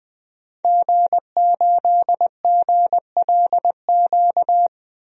Morse code, heard as a keyed tone, about 21 wpm, 700 Hz. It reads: G8GLQ
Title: DXCC-100 Call Signs - 5 Characters - 20wpm